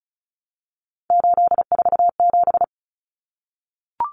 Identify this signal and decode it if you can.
Morse code, heard as a keyed tone, about 34 wpm, 700 Hz. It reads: 847
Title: Numbers - 3-Digit 35wpm